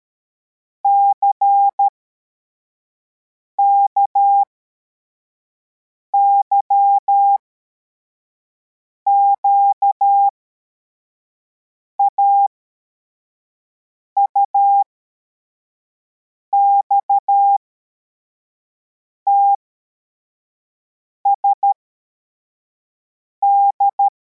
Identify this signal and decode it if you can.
Morse code, heard as a keyed tone, about 13 wpm, spaced as for 4 wpm, 800 Hz. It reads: CKYQA UXTSD
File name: formation / cw / Text74